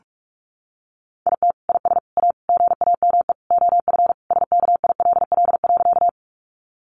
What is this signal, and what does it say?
Morse code, heard as a keyed tone, about 44 wpm, 700 Hz. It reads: IT IS A GAME OF SKILL.
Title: Sentences from Top 1200 Words 45wpm